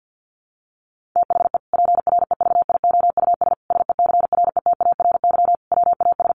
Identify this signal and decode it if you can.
Morse code, heard as a keyed tone, about 50 wpm, 700 Hz. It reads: THE PREVIOUS SECRETARY WAS
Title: Sentences from Top 1500 Words 50wpm